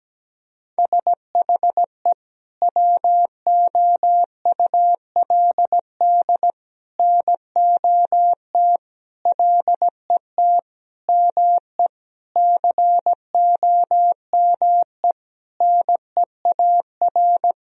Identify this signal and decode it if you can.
Morse code, heard as a keyed tone, about 17 wpm, 700 Hz. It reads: SHE WOULD NOT LET ME COME NEAR
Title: Sentences from Top 1000 Words - Review Part 1 - 17wpm